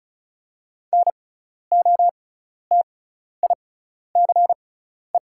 Morse code, heard as a keyed tone, about 36 wpm, spaced as for 12 wpm, 690 Hz. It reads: NOTICE
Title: 6x Character Spacing - Top 500 Words - 35wpm